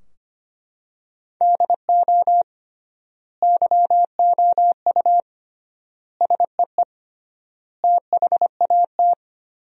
Morse code, heard as a keyed tone, about 25 wpm, 700 Hz. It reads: DO YOU SEE THAT
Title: 3x Word-Spacing - Sentences from Top 500 Words - 25wpm